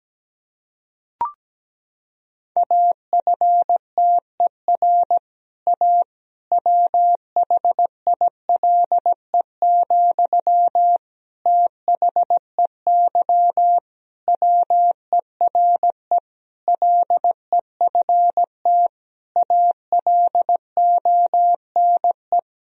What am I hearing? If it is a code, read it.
Morse code, heard as a keyed tone, about 17 wpm, 700 Hz. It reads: AFTER A WHILE, THEY WERE LEFT ALONE
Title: Sentences from Top 1000 Words - Review Part 1 - 17wpm